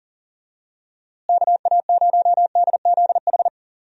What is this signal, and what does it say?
Morse code, heard as a keyed tone, about 40 wpm, 700 Hz. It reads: KA0DZH
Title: USA Call Signs - 6 Characters - Repeated 3x - 40wpm